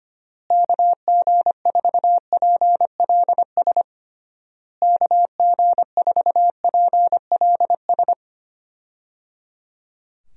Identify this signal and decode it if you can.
Morse code, heard as a keyed tone, about 25 wpm, 700 Hz. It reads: KG4PLS KG4PLS
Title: USA Call Signs - 6 Characters - Repeated 3x - 25wpm